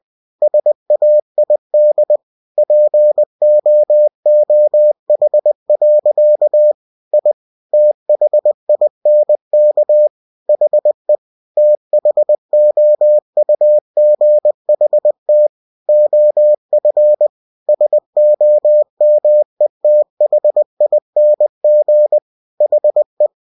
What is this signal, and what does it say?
Morse code, heard as a keyed tone, about 20 wpm, 600 Hz. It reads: SAID POOH. I THINK HE THOUGHT OF SOMETHING HE
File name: cw-books / winnie-the-pooh / winnie-20wpm-page-0070